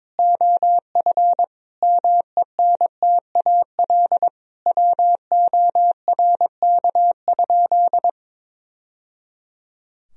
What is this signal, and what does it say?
Morse code, heard as a keyed tone, about 22 wpm, 700 Hz. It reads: OF MENTAL WORK?
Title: Sentences from Top 1000 Words - Review Part 4 - 22wpm